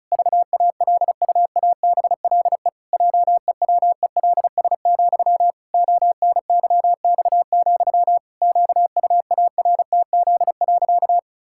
Morse code, heard as a keyed tone, about 34 wpm, 700 Hz. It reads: VALUABLE JEWELS, ONYX, QUARTZ.
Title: Pangram Challenge v2 - 35wpm